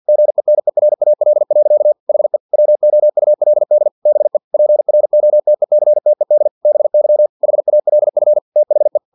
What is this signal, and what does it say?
Morse code, heard as a keyed tone, about 50 wpm, 600 Hz. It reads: GENERAL. HE WOULD BE PROTECTED BY HALF THE
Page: Never 49wpm